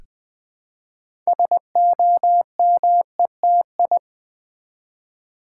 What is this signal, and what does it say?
Morse code, heard as a keyed tone, about 20 wpm, 700 Hz. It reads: SOMETI
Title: Build A Word - 3 Letters At A Time - 9-Letter Words - 20wpm